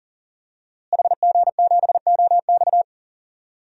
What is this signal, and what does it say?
Morse code, heard as a keyed tone, about 40 wpm, 700 Hz. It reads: HG7OX
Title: DXCC-100 Call Signs - 5 Characters - Repeated 2x - 40wpm